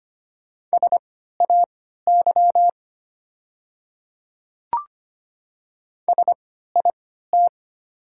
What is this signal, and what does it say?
Morse code, heard as a keyed tone, about 25 wpm, spaced as for 14 wpm, 690 Hz. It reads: SAY SIT
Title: 3x Character Spacing - Top 500 Words - 25wpm